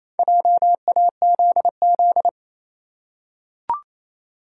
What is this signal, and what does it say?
Morse code, heard as a keyed tone, about 27 wpm, 700 Hz. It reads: JAZZ
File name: Easy Words - Speed-Racing with Hints - 22wpm